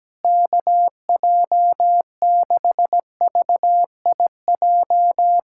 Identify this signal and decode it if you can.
Morse code, heard as a keyed tone, about 17 wpm, 700 Hz. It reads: KJ6VIJ
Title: USA Callsign DE Callsign - 6 Characters - 17wpm